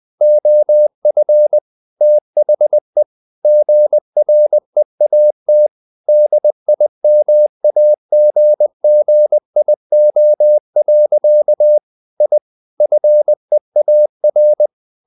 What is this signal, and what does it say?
Morse code, heard as a keyed tone, about 20 wpm, 600 Hz. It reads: OF THE GREAT DIMAGGIO. I FEAR
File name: the old man and the sea / the old man and the sea part 1 20wpm